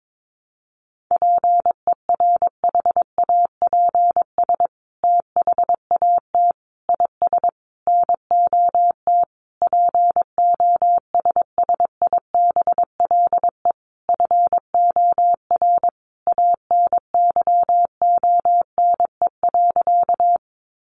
Morse code, heard as a keyed tone, about 22 wpm, 700 Hz. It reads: PERHAPS THAT IS NOT POSSIBLE FOR ANYONE.
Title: Sentences from Top 900 Words 22wpm